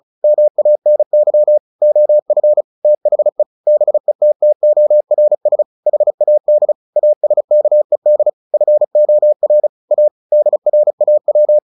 Morse code, heard as a keyed tone, about 34 wpm, 600 Hz. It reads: MANY OF THE BETTORS HAD ASKED FOR A DRAW
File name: the old man and the sea 2 35wpm